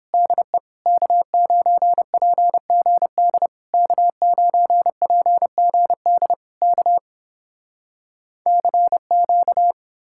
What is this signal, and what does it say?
Morse code, heard as a keyed tone, about 31 wpm, 700 Hz. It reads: DE K9PGD K9PGD K CQ